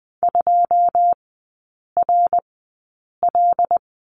Morse code, heard as a keyed tone, about 20 wpm, spaced as for 8 wpm, 690 Hz. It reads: 2RL
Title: 081 Lesson 14e - Number 2 - Three Characters 20wpm